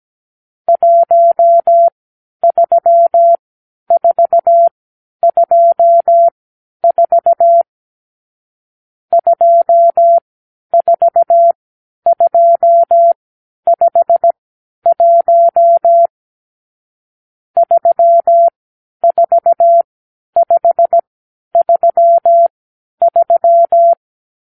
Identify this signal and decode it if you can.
Morse code, heard as a keyed tone, about 17 wpm, spaced as for 10 wpm, 680 Hz. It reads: 13424 24251 34533
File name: arrl / Review (1-5)